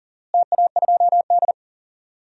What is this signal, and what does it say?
Morse code, heard as a keyed tone, about 40 wpm, 700 Hz. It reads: TA2D
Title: DXCC-100 Call Signs - 4 Characters - Repeated 2x - 40wpm